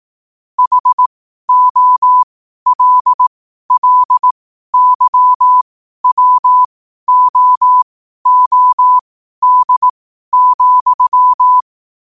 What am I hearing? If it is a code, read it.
Morse code, heard as a keyed tone, about 18 wpm, spaced as for 12 wpm, 1000 Hz. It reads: HOLLYWOOD,
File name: qotdfiles / 12wpm